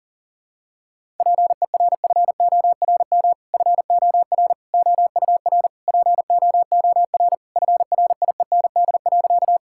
Morse code, heard as a keyed tone, about 40 wpm, 700 Hz. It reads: PERFORM FOR OUR POOR FRIEND.
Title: Sentences from Top 1000 Words - Review Part 5 - 40wpm